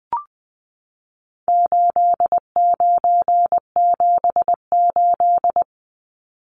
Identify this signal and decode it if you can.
Morse code, heard as a keyed tone, about 20 wpm, 700 Hz. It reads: 8978
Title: Numbers - 4-Digit 20wpm